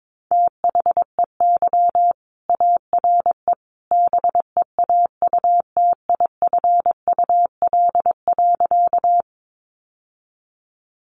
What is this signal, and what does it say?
Morse code, heard as a keyed tone, about 22 wpm, 700 Hz. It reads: THEY ARE BEAUTIFUL.
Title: Sentences from Top 900 Words 22wpm